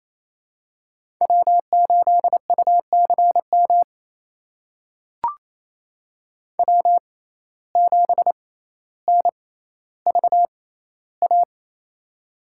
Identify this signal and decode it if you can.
Morse code, heard as a keyed tone, about 28 wpm, spaced as for 14 wpm, 700 Hz. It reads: W8UCM W7NVA
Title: USA Call Signs - 5 Characters - 6x Character Spacing - 28wpm